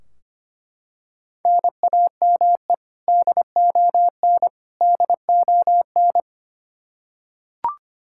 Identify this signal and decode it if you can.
Morse code, heard as a keyed tone, about 25 wpm, 700 Hz. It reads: NAME DON DON
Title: Practice QSO - Exchange 1 - Beginning - 25wpm